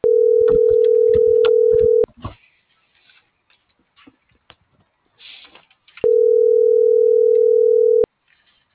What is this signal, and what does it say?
Ringback tone (North America), 440 + 480 Hz.